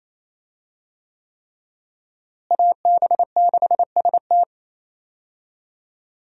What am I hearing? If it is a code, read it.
Morse code, heard as a keyed tone, about 28 wpm, 700 Hz. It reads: AB6ST
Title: USA Call Signs - 5 Characters - Repeated 3x - 28wpm